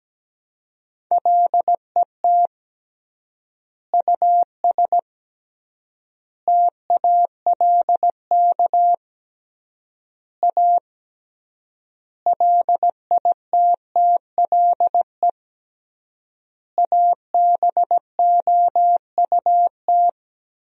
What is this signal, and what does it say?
Morse code, heard as a keyed tone, about 17 wpm, 700 Hz. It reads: LET US TALK A LITTLE ABOUT
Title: 3x Word-Spacing - Sentences from Top 400 Words - 17wpm